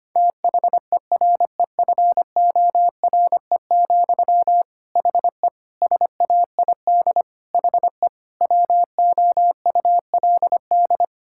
Morse code, heard as a keyed tone, about 25 wpm, 700 Hz. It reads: THEREFORE, HE SAID HE WOULD
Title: Sentences from Top 1000 Words - Review Part 5 - 25wpm